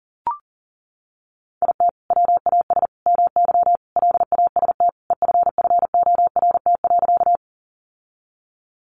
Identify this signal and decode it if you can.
Morse code, heard as a keyed tone, about 37 wpm, 700 Hz. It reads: IT WAS MY LAST EFFORT.
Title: Sentences from Top 1200 Words 40wpm